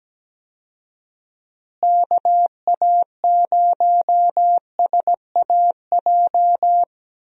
Morse code, heard as a keyed tone, about 17 wpm, 700 Hz. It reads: KA0SAJ
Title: USA Call Signs - 6 Characters - Repeated 3x - 17wpm